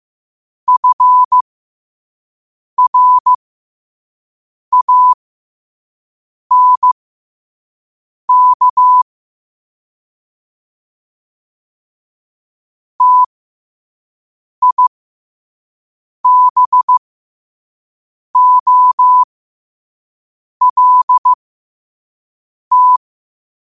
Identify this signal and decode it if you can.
Morse code, heard as a keyed tone, about 15 wpm, spaced as for 5 wpm, 1000 Hz. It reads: FRANK TIBOLT
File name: qotdfiles / 5wpm